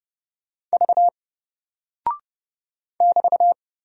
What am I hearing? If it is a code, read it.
Morse code, heard as a keyed tone, about 30 wpm, 700 Hz. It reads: V <BT>
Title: Single Character (Letter:Number:Punctuation:Prosign) Alt 30wpm